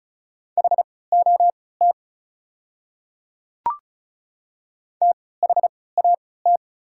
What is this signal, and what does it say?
Morse code, heard as a keyed tone, about 36 wpm, spaced as for 20 wpm, 700 Hz. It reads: HOT THAT
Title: 3x Character Spacing - Top 200 Words - 35wpm